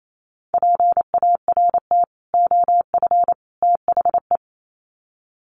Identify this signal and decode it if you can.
Morse code, heard as a keyed tone, about 28 wpm, 700 Hz. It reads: PART OF THE
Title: Sets of 3 Words 28wpm